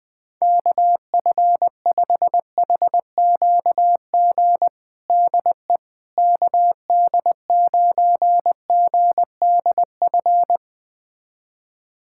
Morse code, heard as a keyed tone, about 20 wpm, 700 Hz. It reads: KF5HQG DE KD9GDF
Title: USA Callsign DE Callsign 20wpm